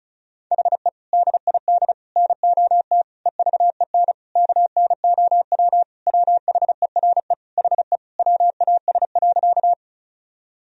Morse code, heard as a keyed tone, about 36 wpm, 700 Hz. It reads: HE DID NOT EVEN KNOW WHERE HE WAS.
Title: Sentences from Top 1000 Words - Review Part 1 - 35wpm